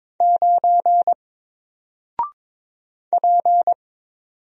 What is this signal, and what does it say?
Morse code, heard as a keyed tone, about 22 wpm, 700 Hz. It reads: 9 P
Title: Single Character (Letter:Number:Punctuation:Prosign) 22wpm